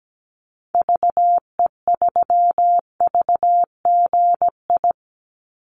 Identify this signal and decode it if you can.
Morse code, heard as a keyed tone, about 17 wpm, 700 Hz. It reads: VE3VGI
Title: Call Signs Repeated 2x - 17wpm